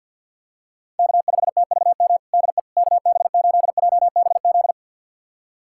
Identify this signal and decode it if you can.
Morse code, heard as a keyed tone, about 52 wpm, 700 Hz. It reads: K5TVM DE KB8JBZ
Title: USA Callsign DE Callsign 50wpm